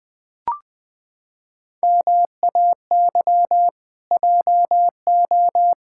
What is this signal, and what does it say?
Morse code, heard as a keyed tone, about 21 wpm, 700 Hz. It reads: MAY JO
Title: Pangram Challenge v2 - 20wpm